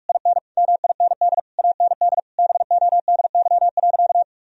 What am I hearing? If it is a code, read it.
Morse code, heard as a keyed tone, about 42 wpm, 700 Hz. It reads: IN MIND AND BODY.
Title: Sentences from Top 1000 Words - Review Part 2 - 45wpm